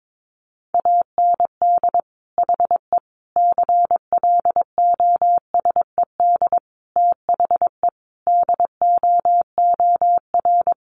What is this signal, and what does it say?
Morse code, heard as a keyed tone, about 22 wpm, 700 Hz. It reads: AND HE CLOSED THE DOOR
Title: Sentences from Top 1500 Words 22wpm